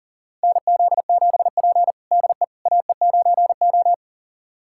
Morse code, heard as a keyed tone, about 40 wpm, 700 Hz. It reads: NZ7P DE AE9O
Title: USA Callsign DE Callsign - 4 Characters - 40wpm